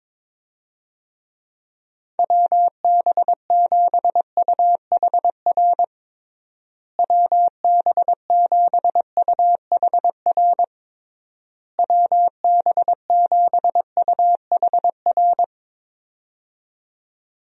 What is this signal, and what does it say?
Morse code, heard as a keyed tone, about 22 wpm, 700 Hz. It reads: WB7UHR WB7UHR WB7UHR
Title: USA Call Signs - 6 Characters - Repeated 3x - 22wpm